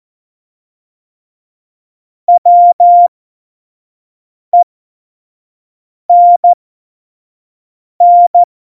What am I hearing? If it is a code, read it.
Morse code, heard as a keyed tone, about 14 wpm, spaced as for 5 wpm, 700 Hz. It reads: WENN
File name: Morsekurs / Witze / Witz 20. November 2018, 25 F70